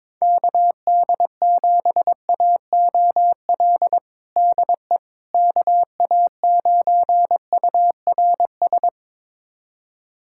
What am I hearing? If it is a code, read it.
Morse code, heard as a keyed tone, about 22 wpm, 700 Hz. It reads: KD7AOL DE KA9URS
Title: USA Callsign DE Callsign 22wpm